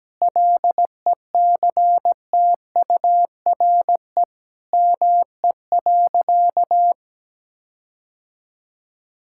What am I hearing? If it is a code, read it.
Morse code, heard as a keyed tone, about 17 wpm, 700 Hz. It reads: LECTURE ME.
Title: Sentences from Top 1600 Words 17wpm